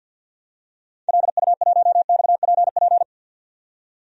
Morse code, heard as a keyed tone, about 46 wpm, 700 Hz. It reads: LU1XPP
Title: DXCC-100 Call Signs - 6 Characters - 50wpm